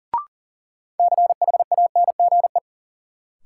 Morse code, heard as a keyed tone, about 40 wpm, 700 Hz. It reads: CHANGE
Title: Sentences from Top 600 Words - New Word Drill - 40wpm